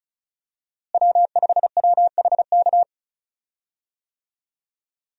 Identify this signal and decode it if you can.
Morse code, heard as a keyed tone, about 34 wpm, 700 Hz. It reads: W5WHK
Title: USA Call Signs - 5 Characters - Repeated 3x - 35wpm